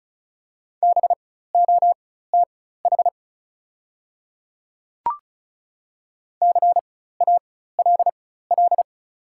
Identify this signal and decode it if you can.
Morse code, heard as a keyed tone, about 36 wpm, spaced as for 16 wpm, 690 Hz. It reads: BOTH CALL
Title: 4x Character Spacing - Top 300 Words - 35wpm